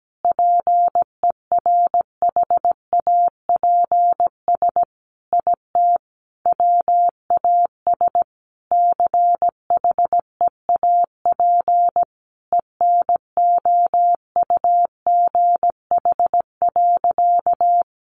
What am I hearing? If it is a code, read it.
Morse code, heard as a keyed tone, about 17 wpm, 700 Hz. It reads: PERHAPS IT WAS CHEAP ENOUGH.
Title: Sentences from Top 1200 Words 17wpm